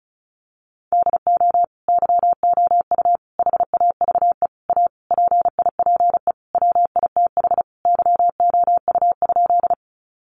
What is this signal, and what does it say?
Morse code, heard as a keyed tone, about 34 wpm, 700 Hz. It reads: DO YOU HAVE A PIPE WITH YOU?
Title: Sentences from Top 1500 Words 35wpm